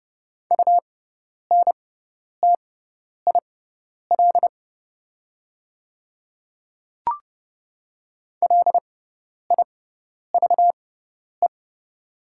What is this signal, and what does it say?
Morse code, heard as a keyed tone, about 30 wpm, spaced as for 10 wpm, 690 Hz. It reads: UNTIL LIVE
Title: 6x Character Spacing - Top 300 Words - 30wpm